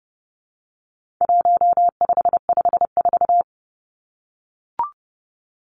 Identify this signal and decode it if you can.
Morse code, heard as a keyed tone, about 30 wpm, 700 Hz. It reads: 1554
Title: Numbers - 4-Digit 30wpm